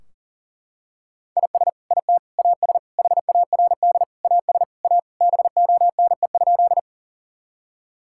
Morse code, heard as a keyed tone, about 37 wpm, 700 Hz. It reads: IS IT AS HARD AS A BONE?
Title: Sentences from Top 1600 Words 40wpm